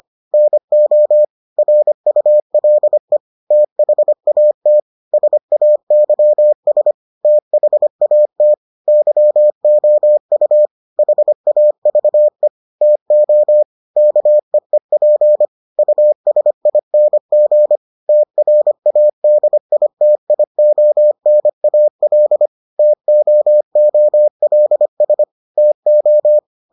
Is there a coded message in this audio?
Yes: Morse code from 0.3 to 26.4 s, heard as a keyed tone, about 25 wpm, 600 Hz. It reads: NO RULE THAT SAYS THAT YOU HAVE TO KEEP USING TRADITIONAL TOOLS TO